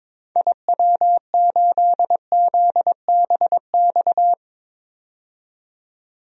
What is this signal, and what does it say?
Morse code, heard as a keyed tone, about 22 wpm, 700 Hz. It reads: IW8ZBX
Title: DXCC-100 Call Signs - 6 Characters - 22wpm